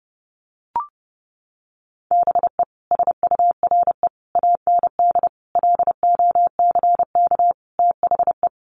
Morse code, heard as a keyed tone, about 30 wpm, 700 Hz. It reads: BE SURE AND LOCK THE
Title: Sentences from Top 1300 Words 30wpm